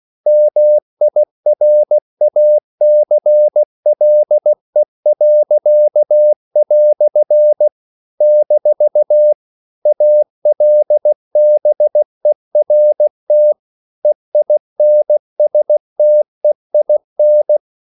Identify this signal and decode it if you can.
Morse code, heard as a keyed tone, about 16 wpm, 600 Hz. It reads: MIRACLE." - ALBERT EINSTEIN
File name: CW / Practice / EnglishQuotations0416